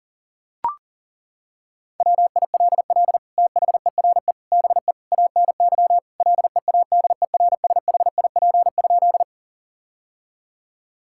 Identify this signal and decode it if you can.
Morse code, heard as a keyed tone, about 40 wpm, 700 Hz. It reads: WILL THERE BE ANY LEADERSHIP?
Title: Sentences from Top 1600 Words 40wpm